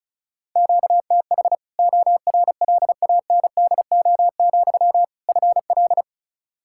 Morse code, heard as a keyed tone, about 36 wpm, 700 Hz. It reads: QTH ORLANDO, FL
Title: Practice QSO - Exchange 1 - Beginning - 35wpm